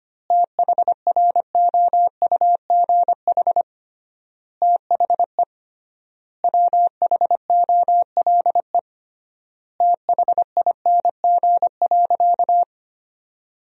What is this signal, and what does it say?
Morse code, heard as a keyed tone, about 25 wpm, 700 Hz. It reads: THROUGH THE WHOLE THING.
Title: 3x Word-Spacing - Sentences from Top 400 Words - 25wpm